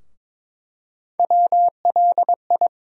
Morse code, heard as a keyed tone, about 22 wpm, 700 Hz. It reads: WLI
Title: USA Call Sign Postfixes - 3 Letters 22wpm